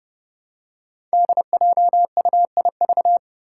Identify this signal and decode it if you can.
Morse code, heard as a keyed tone, about 30 wpm, 700 Hz. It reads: DJUIV
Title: Five Challenging Letter-Number Combinations 30wpm